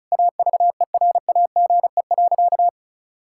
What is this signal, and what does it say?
Morse code, heard as a keyed tone, about 36 wpm, 700 Hz. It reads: AVERAGE.
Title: Sentences from Top 1000 Words - Review Part 6 - 35wpm